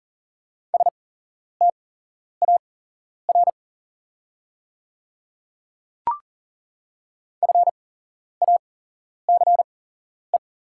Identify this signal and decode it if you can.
Morse code, heard as a keyed tone, about 40 wpm, spaced as for 11 wpm, 700 Hz. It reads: STAR FACE
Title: 8x Character Spacing - Top 500 Words - 40wpm